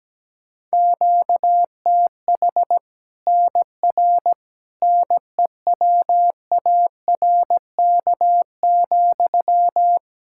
Morse code, heard as a keyed tone, about 17 wpm, 700 Hz. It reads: QTH NR NEWARK,
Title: Practice QSO - Exchange 1 - Beginning - 17wpm